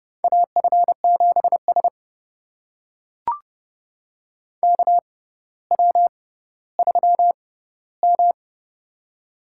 Morse code, heard as a keyed tone, about 29 wpm, spaced as for 15 wpm, 700 Hz. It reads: AF7S KW3M
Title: USA Call Signs - 4 Characters - 6x Character Spacing - 30wpm